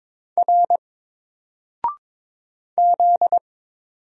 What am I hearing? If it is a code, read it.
Morse code, heard as a keyed tone, about 22 wpm, 700 Hz. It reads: R Z
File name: Single-Letter-Number-22wpm